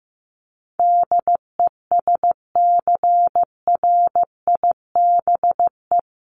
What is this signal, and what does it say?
Morse code, heard as a keyed tone, about 15 wpm, 700 Hz. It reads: DESCRIBE
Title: Sentences from Top 1000 Words 15wpm